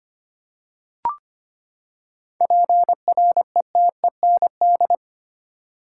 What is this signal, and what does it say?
Morse code, heard as a keyed tone, about 25 wpm, 700 Hz. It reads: PRETEND
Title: Seven Letter Words - 25wpm